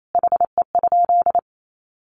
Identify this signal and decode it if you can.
Morse code, heard as a keyed tone, about 28 wpm, 700 Hz. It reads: HE?
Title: Sentences from Top 1300 Words 28wpm